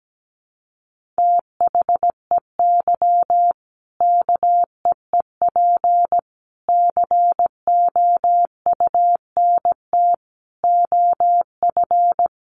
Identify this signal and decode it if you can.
Morse code, heard as a keyed tone, about 17 wpm, 700 Hz. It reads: THEY KEEP COUNT OF
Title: Sentences from Top 1500 Words 17wpm